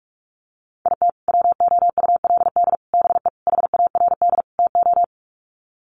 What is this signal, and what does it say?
Morse code, heard as a keyed tone, about 44 wpm, 700 Hz. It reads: IT WOULD BE HARD TO
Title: Sets of 5 Words Encore2 45wpm